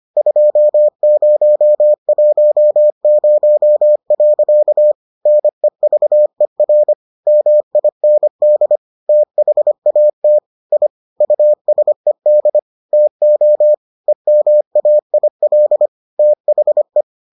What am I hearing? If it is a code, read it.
Morse code, heard as a keyed tone, about 25 wpm, 600 Hz. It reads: 2010. NEVER MIND THAT I USED TO EMAIL THE